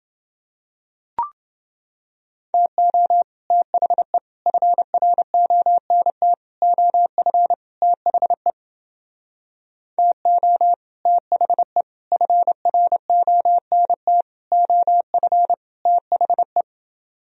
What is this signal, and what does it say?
Morse code, heard as a keyed tone, about 29 wpm, 700 Hz. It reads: TO THE FRONT OF THE TO THE FRONT OF THE